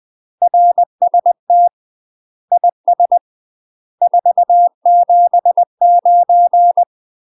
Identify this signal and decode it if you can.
Morse code, heard as a keyed tone, about 20 wpm, 700 Hz. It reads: RST IS 479
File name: UK MASTER QSOs TRAINING / 20WPM / UK MASTER QSOs 0064